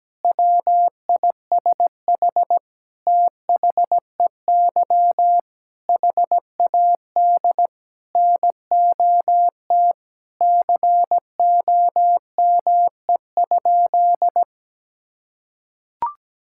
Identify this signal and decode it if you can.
Morse code, heard as a keyed tone, about 17 wpm, 700 Hz. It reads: WISH THEY HAD NOT COME?
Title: Sentences from Top 1600 Words 17wpm